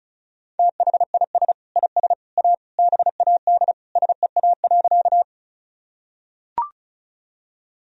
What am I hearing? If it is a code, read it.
Morse code, heard as a keyed tone, about 36 wpm, 700 Hz. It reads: THIS IS A BAD SEA.
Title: Sentences from Top 1000 Words - Review Part 1 - 35wpm